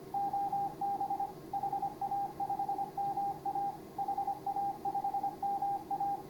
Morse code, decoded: OB HA5MA HA5MA

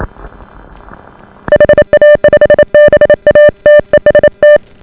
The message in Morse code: HA5BA TEST

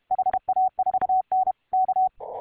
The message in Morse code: HA4N K

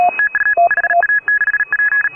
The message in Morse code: TU HA6P

